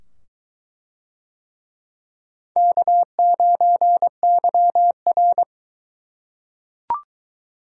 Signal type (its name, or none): Morse code